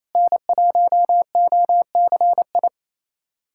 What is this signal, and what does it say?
Morse code, heard as a keyed tone, about 27 wpm, 700 Hz. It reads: N1OCI
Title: USA Callsign DE Callsign 28wpm